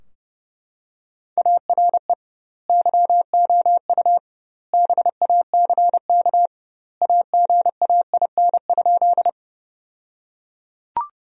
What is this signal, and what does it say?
Morse code, heard as a keyed tone, about 30 wpm, 700 Hz. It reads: ARE YOU BACK AGAIN?
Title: 2x Word-Spacing - Sentences from Top 200 Words - 30wpm